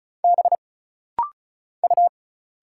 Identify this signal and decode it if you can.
Morse code, heard as a keyed tone, about 36 wpm, 700 Hz. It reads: B U
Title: Single Letters - Rapid-Fire 35wpm